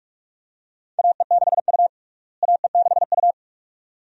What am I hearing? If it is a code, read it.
Morse code, heard as a keyed tone, about 48 wpm, 700 Hz. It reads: AE6U AE6U
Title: USA Call Signs - 4 Characters - Repeated 2x - 45wpm